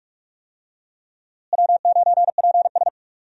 Morse code, heard as a keyed tone, about 41 wpm, 700 Hz. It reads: W9PS
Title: USA Call Signs - 4 Characters - Repeated 3x - 45wpm